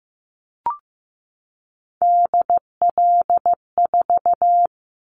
Morse code, heard as a keyed tone, about 15 wpm, 700 Hz. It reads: DL4
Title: Callsign Prefixes for Top 50 DXCC - 15wpm